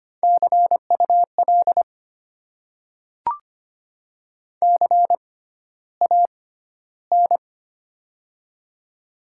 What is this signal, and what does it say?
Morse code, heard as a keyed tone, about 25 wpm, spaced as for 13 wpm, 700 Hz. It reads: CUL CAN
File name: Top 90 QSO Elements - 6x Character Spacing - 25wpm